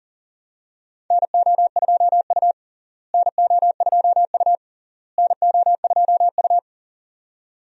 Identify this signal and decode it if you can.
Morse code, heard as a keyed tone, about 37 wpm, 700 Hz. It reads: NO2U NO2U NO2U
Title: USA Call Signs - 4 Characters - Repeated 3x - 40wpm